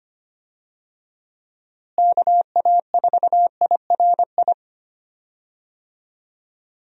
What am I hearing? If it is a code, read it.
Morse code, heard as a keyed tone, about 25 wpm, 700 Hz. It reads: KA4IRI
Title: USA Call Signs - 6 Characters - Repeated 2x - 25wpm